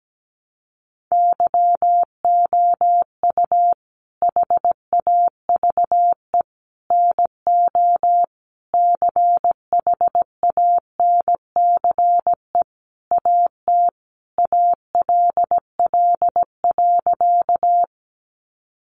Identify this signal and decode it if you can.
Morse code, heard as a keyed tone, about 17 wpm, 700 Hz. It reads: YOU HAVE NO CHANCE AT ALL.
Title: Sentences from Top 800 Words 17wpm